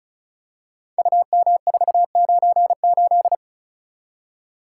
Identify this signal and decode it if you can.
Morse code, heard as a keyed tone, about 36 wpm, 700 Hz. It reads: UM498
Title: Code Groups - 5 Letter:Numbers - 35wpm